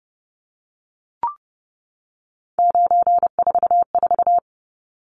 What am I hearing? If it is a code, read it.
Morse code, heard as a keyed tone, about 30 wpm, 700 Hz. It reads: 944
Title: Numbers - 3-Digit 30wpm